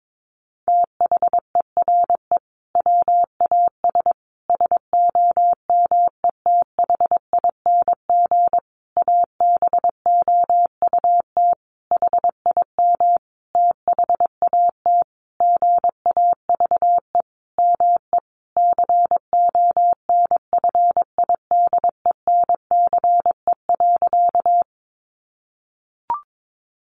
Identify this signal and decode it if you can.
Morse code, heard as a keyed tone, about 22 wpm, 700 Hz. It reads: THERE WAS SOMETHING ABOUT HIM THAT GAVE ME CONFIDENCE.
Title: Sentences from Top 1500 Words 22wpm